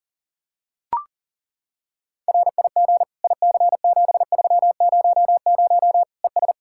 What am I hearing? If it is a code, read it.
Morse code, heard as a keyed tone, about 40 wpm, 700 Hz. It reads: RIG IC7300 ES
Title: Practice QSO - Exchange 2 - Info - 40wpm